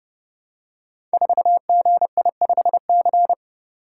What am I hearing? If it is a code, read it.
Morse code, heard as a keyed tone, about 30 wpm, 700 Hz. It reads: 4GI5C
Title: Code Groups - 5 Letter:Numbers - 30wpm